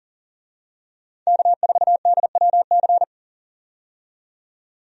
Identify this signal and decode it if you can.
Morse code, heard as a keyed tone, about 40 wpm, 700 Hz. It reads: K4DWC
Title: USA Call Signs - 5 Characters - 40wpm